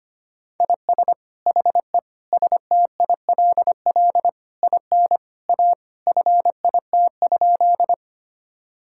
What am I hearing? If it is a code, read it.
Morse code, heard as a keyed tone, about 25 wpm, 700 Hz. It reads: IS HE STILL IN A FIT?
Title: Sentences from Top 1000 Words - Review Part 5 - 25wpm